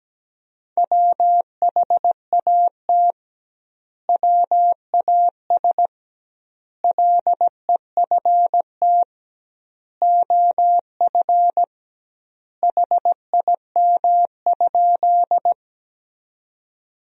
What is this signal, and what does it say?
Morse code, heard as a keyed tone, about 17 wpm, 700 Hz. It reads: WHAT WAS LEFT OF HIM?
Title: 2x Word-Spacing - Sentences from Top 300 Words - 17wpm